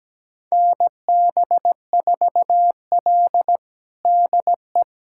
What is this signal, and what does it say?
Morse code, heard as a keyed tone, about 17 wpm, 700 Hz. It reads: NB4L DE
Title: USA Callsign DE Callsign - 4 Characters - 17wpm